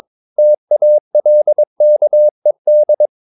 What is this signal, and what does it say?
Morse code, heard as a keyed tone, about 22 wpm, 600 Hz. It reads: TALKED
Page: Romanen Never af Ken Follet på engelsk.